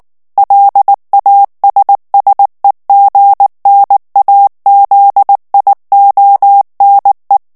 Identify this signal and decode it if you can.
Morse code, heard as a keyed tone, about 19 wpm, 800 Hz. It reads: LASSEGNAZIONE
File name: morse mp3 / ITALIEN texte clair 2 vitesse 18 mots minute